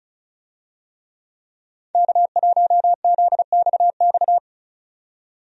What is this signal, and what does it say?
Morse code, heard as a keyed tone, about 36 wpm, 700 Hz. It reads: K1ZXX
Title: USA Call Signs - 5 Characters - 35wpm